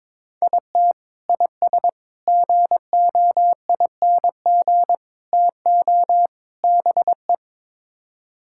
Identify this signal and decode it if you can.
Morse code, heard as a keyed tone, about 22 wpm, 700 Hz. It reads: IT IS GOING TO BE
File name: practice / 22wpm / Build A Phrase - 5-Word Phrases - 22wpm